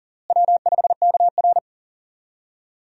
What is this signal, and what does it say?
Morse code, heard as a keyed tone, about 40 wpm, 700 Hz. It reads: W5KR
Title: USA Call Signs - 4 Characters - Repeated 2x - 40wpm